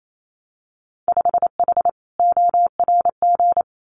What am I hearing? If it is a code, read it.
Morse code, heard as a keyed tone, about 28 wpm, 700 Hz. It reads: 5H ORG
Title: Winter Field Day Exchanges Encore 28wpm